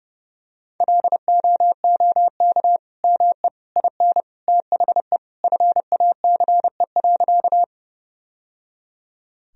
Morse code, heard as a keyed tone, about 30 wpm, 700 Hz. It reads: LOOK ME IN THE FACE.
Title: Sentences from Top 1000 Words - Review Part 2 - 30wpm